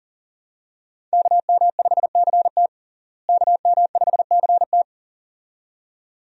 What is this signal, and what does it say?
Morse code, heard as a keyed tone, about 40 wpm, 700 Hz. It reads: KM5CT KM5CT
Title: USA Call Signs - 5 Characters - Repeated 2x - 40wpm